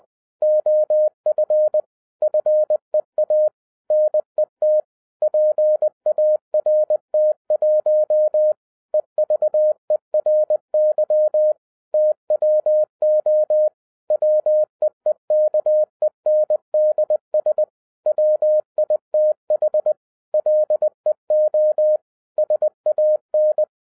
Morse code, heard as a keyed tone, about 20 wpm, 610 Hz. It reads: OF FEA NET PART1 EVERY TWO WEEKENDS WITH LEO SAN
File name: 2012-01-mp3 / n16-THROUGH EIGHT MONTHS-20wpm